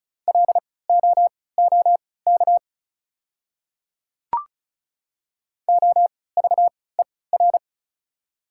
Morse code, heard as a keyed tone, about 34 wpm, spaced as for 20 wpm, 690 Hz. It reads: LOOK OVER
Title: 3x Character Spacing - Top 100 Words - 35wpm